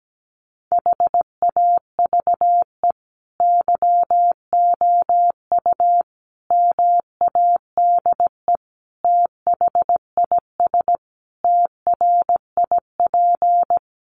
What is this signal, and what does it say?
Morse code, heard as a keyed tone, about 17 wpm, 700 Hz. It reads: HAVE YOU MADE THIS TRIP
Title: Sentences from Top 1300 Words 17wpm